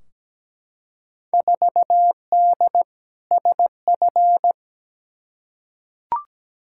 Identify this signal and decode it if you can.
Morse code, heard as a keyed tone, about 17 wpm, 700 Hz. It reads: 4D SF
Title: Field Day Exchanges Encore - 17wpm